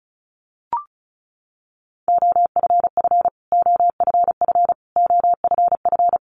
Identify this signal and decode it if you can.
Morse code, heard as a keyed tone, about 34 wpm, 700 Hz. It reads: OFF OFF OFF
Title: Top 300 Words Repeated 35wpm